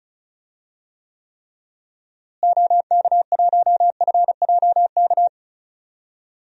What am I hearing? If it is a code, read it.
Morse code, heard as a keyed tone, about 36 wpm, 700 Hz. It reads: OK1FJK
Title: DXCC-100 Call Signs - 6 Characters - Repeated 3x - 35wpm